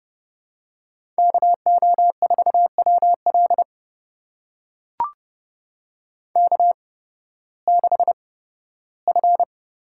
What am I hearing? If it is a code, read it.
Morse code, heard as a keyed tone, about 30 wpm, spaced as for 16 wpm, 700 Hz. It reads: KO4WL K6F
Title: USA Call Signs - 5 Characters - 8x Character Spacing - 30wpm